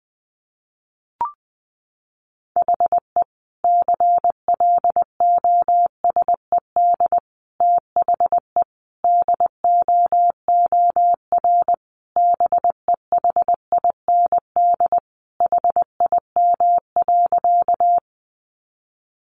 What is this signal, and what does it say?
Morse code, heard as a keyed tone, about 20 wpm, 700 Hz. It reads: HE CLOSED THE DOOR BEHIND HIM.
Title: Sentences from Top 1400 Words 20wpm